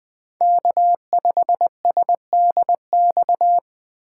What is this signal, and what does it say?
Morse code, heard as a keyed tone, about 24 wpm, 700 Hz. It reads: K5SDX
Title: Speed Racing - USA 1x3 Call Signs 20wpm